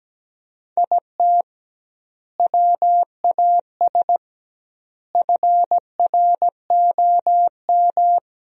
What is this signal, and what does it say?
Morse code, heard as a keyed tone, about 17 wpm, 700 Hz. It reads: IT WAS FROM
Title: 2x Word-Spacing - Sentences from Top 100 Words - 17wpm